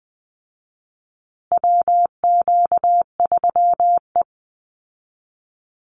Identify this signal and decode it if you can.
Morse code, heard as a keyed tone, about 20 wpm, 700 Hz. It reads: WQ3E
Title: Call Signs Repeated 2x - 20wpm